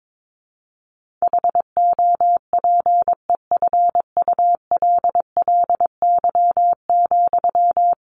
Morse code, heard as a keyed tone, about 22 wpm, 700 Hz. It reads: HOPEFULLY,
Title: Sentences from Top 1500 Words 22wpm